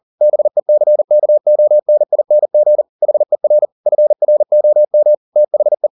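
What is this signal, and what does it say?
Morse code, heard as a keyed tone, about 40 wpm, 600 Hz. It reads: BECKONING HER FROM THE
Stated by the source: Morse code at 40 wpm, practice text, CW med Gnister.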